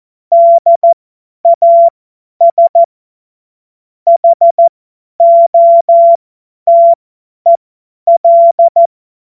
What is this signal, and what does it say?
Morse code, heard as a keyed tone, about 14 wpm, spaced as for 10 wpm, 680 Hz. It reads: DAS HOTEL
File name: Morsekurs / Witze / Witz 21. März 2023 50 F70